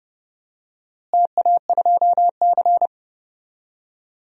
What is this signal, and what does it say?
Morse code, heard as a keyed tone, about 30 wpm, 700 Hz. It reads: TA2C
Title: DXCC-100 Call Signs - 4 Characters - Repeated 3x - 30wpm